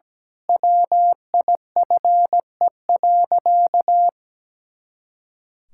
Morse code, heard as a keyed tone, about 17 wpm, 700 Hz. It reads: WIFE.
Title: Pangram Challenge v2 - 17wpm